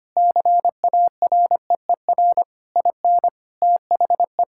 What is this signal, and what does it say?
Morse code, heard as a keyed tone, about 25 wpm, 700 Hz. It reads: CAREER IN THE
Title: Sentences from Top 1000 Words - Review Part 6 - 25wpm